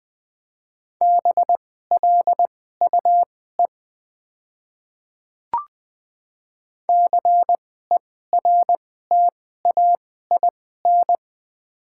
Morse code, heard as a keyed tone, about 21 wpm, spaced as for 15 wpm, 700 Hz. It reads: BLUE CERTAIN
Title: 2x Character Spacing - Top 500 Words - 20wpm